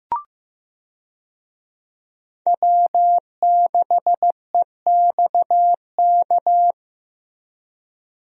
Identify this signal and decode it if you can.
Morse code, heard as a keyed tone, about 15 wpm, 700 Hz. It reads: W6EXK
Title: USA Call Signs - 5 Characters - Repeated 2x - 15wpm